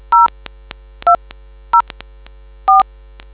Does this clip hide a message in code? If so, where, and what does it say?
Yes: DTMF from 0.1 to 2.8 s, heard as telephone touch tones, tones of about 140 ms. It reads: 0 20 4